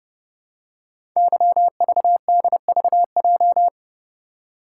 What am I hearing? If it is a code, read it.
Morse code, heard as a keyed tone, about 30 wpm, 700 Hz. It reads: YVDVJ
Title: Code Groups - 5 Letters - 30wpm